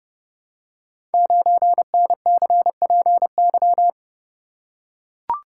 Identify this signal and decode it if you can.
Morse code, heard as a keyed tone, about 30 wpm, 700 Hz. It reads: 9NCPY
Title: Code Groups - 5 Letter:Numbers - 30wpm